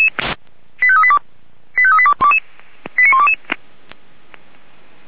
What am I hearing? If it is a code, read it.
Five-tone selective call (ZVEI), heard as quick runs of five tones, tones of about 70 ms. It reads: 85382 85382 89122
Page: SDR/ZVEI